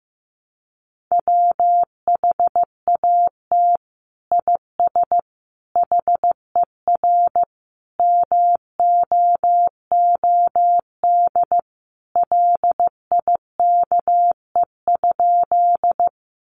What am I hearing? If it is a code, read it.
Morse code, heard as a keyed tone, about 15 wpm, 700 Hz. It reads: WHAT IS HER MOOD LIKE?
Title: Sentences from Top 1100 Words 15wpm